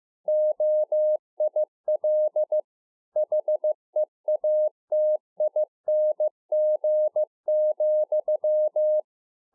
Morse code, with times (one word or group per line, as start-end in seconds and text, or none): 0.3-2.6 OIL
3.2-9.0 HEATING,